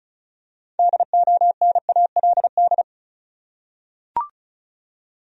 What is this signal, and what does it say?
Morse code, heard as a keyed tone, about 34 wpm, 700 Hz. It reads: DONALD
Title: USA Common Names - 35wpm